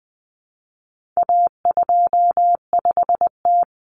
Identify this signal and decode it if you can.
Morse code, heard as a keyed tone, about 20 wpm, 700 Hz. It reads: A25T
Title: Common Cut Numbers - 4-Digit 20wpm